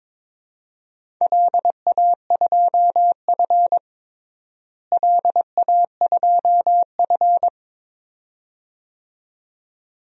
Morse code, heard as a keyed tone, about 22 wpm, 700 Hz. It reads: LA2F LA2F
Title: DXCC-100 Call Signs - 4 Characters - Repeated 2x - 22wpm